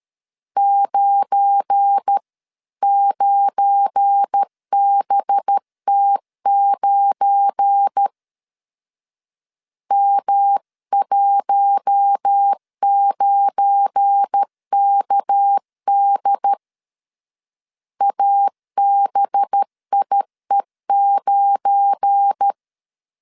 Morse code, heard as a keyed tone, about 13 wpm, 800 Hz. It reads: 9 9BT9 M19KD ABIE9
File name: LEZIONI / 7 GF9Z / 7 GF9 12wpm 2319